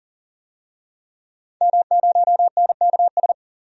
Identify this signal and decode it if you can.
Morse code, heard as a keyed tone, about 40 wpm, 700 Hz. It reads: M0NKS
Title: DXCC-100 Call Signs - 5 Characters - 40wpm